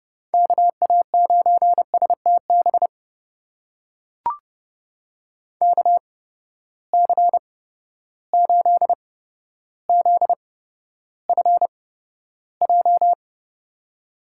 Morse code, heard as a keyed tone, about 30 wpm, spaced as for 13 wpm, 700 Hz. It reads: KA9STB KC8ZFJ
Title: USA Call Signs - 6 Characters - 8x Character Spacing - 30wpm